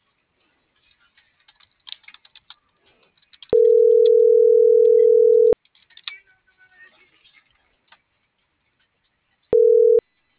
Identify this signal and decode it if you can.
Ringback tone (North America), 440 + 480 Hz.